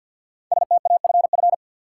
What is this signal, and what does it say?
Morse code, heard as a keyed tone, about 52 wpm, 700 Hz. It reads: STAFF
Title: Sentences from Top 1200 Words - New Word Drill Alt - 50wpm